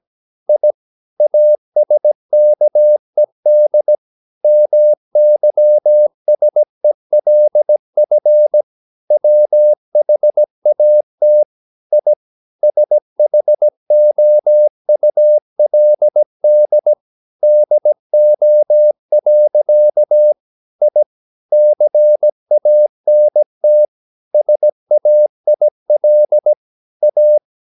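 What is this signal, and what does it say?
Morse code, heard as a keyed tone, about 17 wpm, 600 Hz. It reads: I ASKED MYSELF WHAT I SHOULD DO. I CANT SAIL A
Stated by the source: 17 wpm Morse practice sending the English text of a novel.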